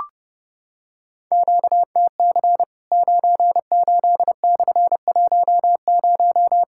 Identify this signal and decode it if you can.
Morse code, heard as a keyed tone, about 30 wpm, 700 Hz. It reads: QTC 98/10
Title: Worked All Europe DX Contest - QTC Exchange - 30wpm